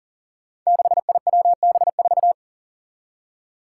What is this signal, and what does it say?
Morse code, heard as a keyed tone, about 40 wpm, 700 Hz. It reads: 6IWB4
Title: Five Challenging Letter-Number Combinations 40wpm